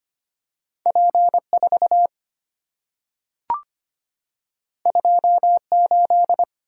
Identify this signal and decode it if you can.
Morse code, heard as a keyed tone, about 25 wpm, 700 Hz. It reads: P4 28
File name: Two Letter-Number Combinations - 25wpm